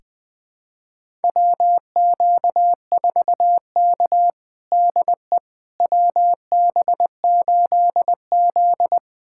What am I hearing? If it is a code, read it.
Morse code, heard as a keyed tone, about 20 wpm, 700 Hz. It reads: WQ4K DE WB8Z
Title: USA Callsign DE Callsign - 4 Characters - 20wpm